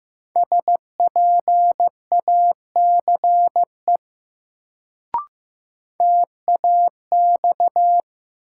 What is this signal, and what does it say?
Morse code, heard as a keyed tone, about 15 wpm, 700 Hz. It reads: SPACE TAX
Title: Sentences from Top 1000 Words - Review Part 5 - New Word Drill - 15wpm